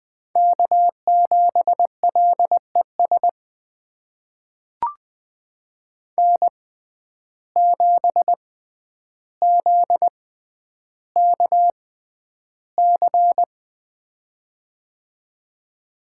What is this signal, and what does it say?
Morse code, heard as a keyed tone, about 20 wpm, spaced as for 10 wpm, 700 Hz. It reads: K7LES N7ZKC
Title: USA Call Signs - 5 Characters - 6x Character Spacing - 20wpm